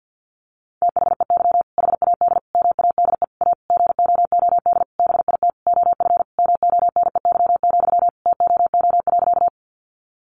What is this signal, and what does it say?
Morse code, heard as a keyed tone, about 49 wpm, 700 Hz. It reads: THEY HAD MADE A GOOD BIT OF MONEY, TOO.
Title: Sentences from Top 900 Words 50wpm